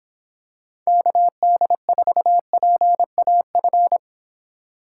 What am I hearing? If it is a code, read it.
Morse code, heard as a keyed tone, about 25 wpm, 700 Hz. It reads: KD4PAF